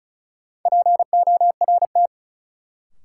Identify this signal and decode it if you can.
Morse code, heard as a keyed tone, about 36 wpm, 700 Hz. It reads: PORT
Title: Sentences from Top 1000 Words - Review Part 1 - New Word Drill - 35wpm